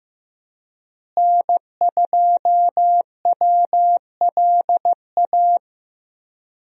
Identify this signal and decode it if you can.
Morse code, heard as a keyed tone, about 15 wpm, 700 Hz. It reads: N2WLA
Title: USA Call Signs - 5 Characters - Repeated 2x - 15wpm